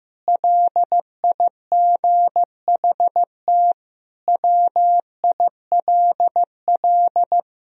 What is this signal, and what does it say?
Morse code, heard as a keyed tone, about 15 wpm, 700 Hz. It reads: LIGHT WILL
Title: Sentences from Top 1000 Words - Review Part 2 - 15wpm